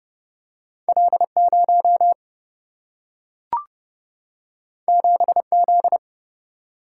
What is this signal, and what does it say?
Morse code, heard as a keyed tone, about 30 wpm, 700 Hz. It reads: L 0 7 Z
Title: Single-Letter Single-Number 30wpm